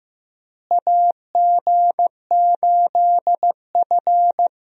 Morse code, heard as a keyed tone, about 15 wpm, 700 Hz. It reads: AG8F
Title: USA Call Signs - 4 Characters - Repeated 3x - 15wpm